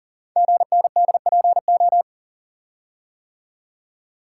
Morse code, heard as a keyed tone, about 41 wpm, 690 Hz. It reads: GNDPO
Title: Code Groups - 5 Letters - 40wpm